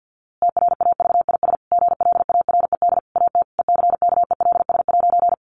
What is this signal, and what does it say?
Morse code, heard as a keyed tone, about 50 wpm, 700 Hz. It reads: TRAVIS GLARED AT ECKELS'
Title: A Sound Of Thunder 50wpm